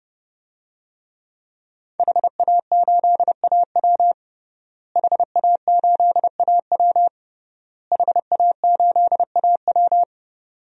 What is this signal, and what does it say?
Morse code, heard as a keyed tone, about 30 wpm, 700 Hz. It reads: HA8AW HA8AW HA8AW
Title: DXCC-100 Call Signs - 5 Characters - Repeated 3x - 30wpm